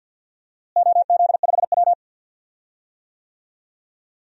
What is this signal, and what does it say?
Morse code, heard as a keyed tone, about 46 wpm, 700 Hz. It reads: OZ5W
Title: DXCC-100 Call Signs - 4 Characters - Repeated 3x - 50wpm